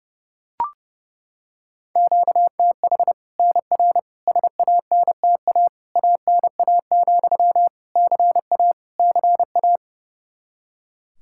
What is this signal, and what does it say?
Morse code, heard as a keyed tone, about 30 wpm, 700 Hz. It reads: QTH NR SANTA ANA, CA CA
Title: Practice QSO - Exchange 1 - Beginning - 30wpm